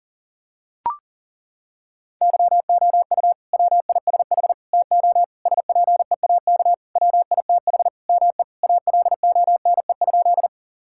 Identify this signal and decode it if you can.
Morse code, heard as a keyed tone, about 40 wpm, 700 Hz. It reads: YOU WISH TO SPEAK WITH ME ALONE?
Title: Sentences from Top 1600 Words 40wpm